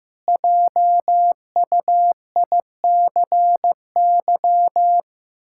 Morse code, heard as a keyed tone, about 15 wpm, 700 Hz. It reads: JUICY
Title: Pangram Challenge v2 - 15wpm